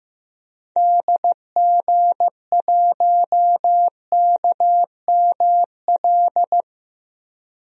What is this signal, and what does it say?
Morse code, heard as a keyed tone, about 15 wpm, 700 Hz. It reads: DG1KML
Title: DXCC-100 Call Signs - 6 Characters - Repeated 3x - 15wpm